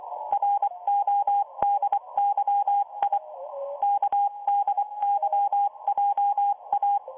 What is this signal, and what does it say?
Morse code, heard as a keyed tone, about 24 wpm, 810 Hz, with radio noise. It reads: RODYI KDYJA